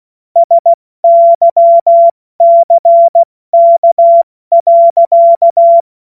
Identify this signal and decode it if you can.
Morse code, heard as a keyed tone, about 16 wpm, spaced as for 14 wpm, 680 Hz. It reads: SYCK.
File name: Witz-23-September-2025-70-F80